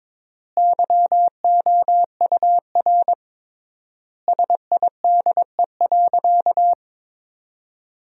Morse code, heard as a keyed tone, about 22 wpm, 700 Hz. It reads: YOUR SIDE.
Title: 3x Word-Spacing - Sentences from Top 400 Words - 22wpm